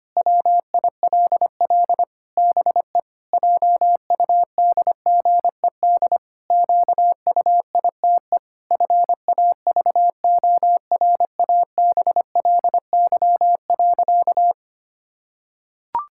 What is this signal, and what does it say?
Morse code, heard as a keyed tone, about 25 wpm, 700 Hz. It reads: WILL BE JUDGED QUITE FAVORABLY.
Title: Pangram Challenge v2 - 25wpm